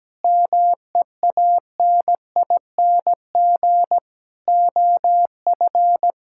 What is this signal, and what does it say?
Morse code, heard as a keyed tone, about 17 wpm, 700 Hz. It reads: MEANING OF
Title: Sentences from Top 1000 Words - Review Part 6 - 17wpm